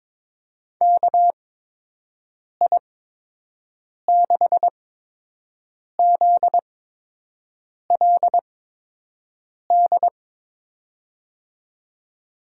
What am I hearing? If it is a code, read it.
Morse code, heard as a keyed tone, about 22 wpm, spaced as for 6 wpm, 700 Hz. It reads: KI6ZLD
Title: USA Call Signs - 6 Characters - 8x Character Spacing - 22wpm